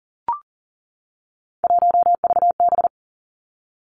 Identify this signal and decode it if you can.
Morse code, heard as a keyed tone, about 42 wpm, 700 Hz. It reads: 1VB
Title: Three Challenging Letter-Number Combinations 40wpm